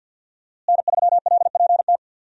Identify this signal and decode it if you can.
Morse code, heard as a keyed tone, about 52 wpm, 700 Hz. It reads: N3LPT
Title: USA Call Signs - 5 Characters - Repeated 2x - 50wpm